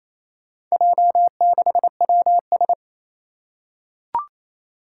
Morse code, heard as a keyed tone, about 28 wpm, 700 Hz. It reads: J6WS
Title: Four Challenging Letter-Number Combinations 28wpm